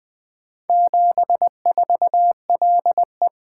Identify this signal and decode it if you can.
Morse code, heard as a keyed tone, about 20 wpm, 700 Hz. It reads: 74LE
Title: Four Letter-Number Combinations - 20wpm